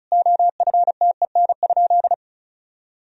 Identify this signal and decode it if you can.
Morse code, heard as a keyed tone, about 36 wpm, 700 Hz. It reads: OFTEN?
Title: Sentences from Top 400 Words and Abbreviations 35wpm